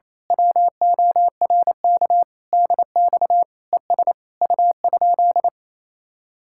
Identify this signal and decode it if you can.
Morse code, heard as a keyed tone, about 26 wpm, 700 Hz. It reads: WORK DX ES U?
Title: Speed-Racing Common QSO Phrases Encore - 28wpm